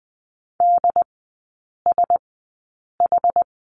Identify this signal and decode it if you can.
Morse code, heard as a keyed tone, about 20 wpm, spaced as for 8 wpm, 700 Hz. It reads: DSH
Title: 074 Lesson 13e - Letter L - Three Characters 20wpm